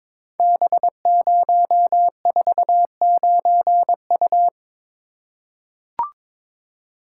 Morse code, heard as a keyed tone, about 22 wpm, 700 Hz. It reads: B049U
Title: Code Groups - 5 Letter:Numbers - 22wpm